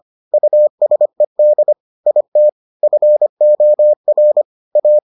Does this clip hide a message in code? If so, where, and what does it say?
Yes: Morse code from 0.3 to 5.0 s, heard as a keyed tone, about 25 wpm, 600 Hz. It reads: USED IT FOR A